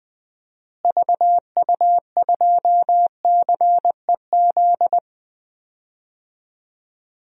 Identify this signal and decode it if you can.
Morse code, heard as a keyed tone, about 20 wpm, 700 Hz. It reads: VU2CEZ
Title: DXCC-100 Call Signs - 6 Characters - Repeated 2x - 20wpm